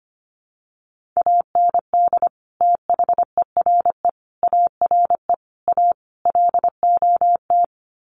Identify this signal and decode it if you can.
Morse code, heard as a keyed tone, about 25 wpm, 700 Hz. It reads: AND THERE ARE A LOT
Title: Sets of 5 Words Encore2 25wpm